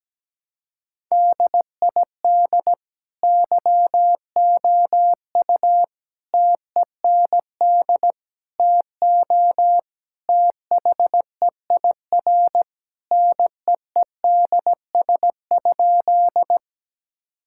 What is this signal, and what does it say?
Morse code, heard as a keyed tone, about 17 wpm, 700 Hz. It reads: DID YOU TEND TO THEIR NEEDS?
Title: Sentences from Top 1000 Words - Review Part 5 - 17wpm